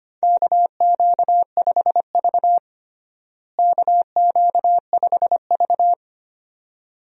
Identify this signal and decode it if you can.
Morse code, heard as a keyed tone, about 25 wpm, 700 Hz. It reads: KQ5V KQ5V
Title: USA Call Signs - 4 Characters - Repeated 2x - 25wpm